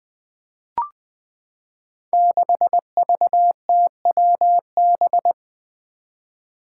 Morse code, heard as a keyed tone, about 20 wpm, 700 Hz. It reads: 6VTWB
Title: Five Challenging Letter-Number Combinations 20wpm